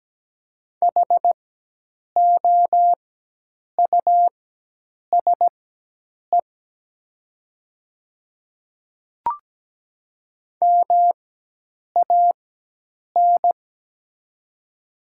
Morse code, heard as a keyed tone, about 17 wpm, spaced as for 8 wpm, 700 Hz. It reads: HOUSE MAN
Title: 4x Character Spacing - Top 300 Words - 17wpm